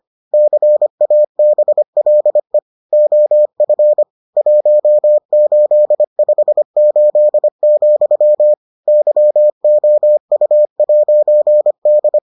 Morse code, heard as a keyed tone, about 25 wpm, 600 Hz. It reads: CABLE OF 1858, YOU'D